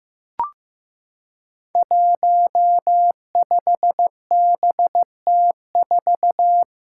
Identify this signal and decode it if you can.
Morse code, heard as a keyed tone, about 15 wpm, 700 Hz. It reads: 15BT4
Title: Five Challenging Letter-Number Combinations 15wpm